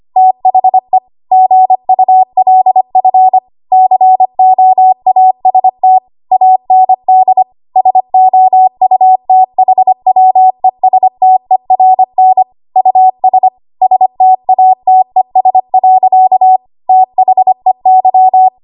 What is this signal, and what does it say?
Morse code, heard as a keyed tone, about 25 wpm, 750 Hz. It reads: THE GULF COAST AND SOUTHWESTERN US STATES. THEY
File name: Morse / 25WPM